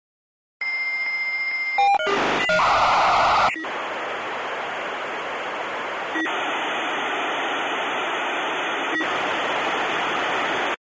Dial-up modem handshake.
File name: Modem connection, signal, interference